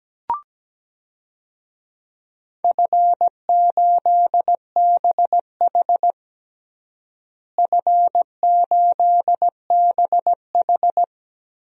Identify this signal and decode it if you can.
Morse code, heard as a keyed tone, about 17 wpm, 700 Hz. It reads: F8BH F8BH
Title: DXCC-100 Call Signs - 4 Characters - Repeated 3x - 17wpm